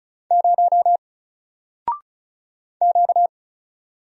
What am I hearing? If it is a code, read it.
Morse code, heard as a keyed tone, about 36 wpm, 700 Hz. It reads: 0 Q
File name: Single-Letter-Number-35wpm